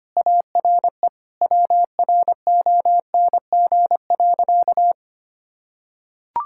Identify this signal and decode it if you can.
Morse code, heard as a keyed tone, about 25 wpm, 700 Hz. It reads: ARE WRONG.
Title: Sentences from Top 1000 Words - Review Part 5 - 25wpm